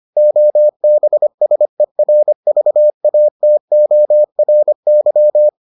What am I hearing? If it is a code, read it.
Morse code, heard as a keyed tone, about 25 wpm, 600 Hz. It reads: OBSERVATORY